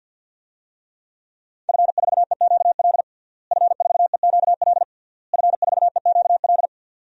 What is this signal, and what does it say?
Morse code, heard as a keyed tone, about 51 wpm, 700 Hz. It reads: F4EQL F4EQL F4EQL
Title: DXCC-100 Call Signs - 5 Characters - Repeated 3x - 50wpm